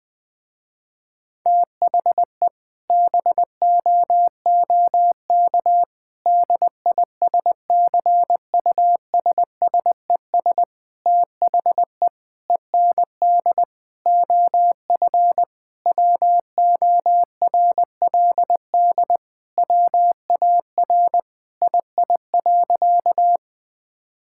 Morse code, heard as a keyed tone, about 19 wpm, 700 Hz. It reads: THE BOOK DISCUSSES THE END OF WORLD WAR II.
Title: Sentences with 6-Word Phrases - Spoken Beginning - 20wpm